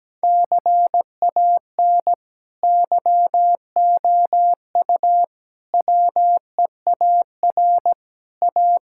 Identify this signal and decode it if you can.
Morse code, heard as a keyed tone, about 17 wpm, 700 Hz. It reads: CAN YOU WEAR A
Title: Sentences from Top 1000 Words - Review Part 6 - 17wpm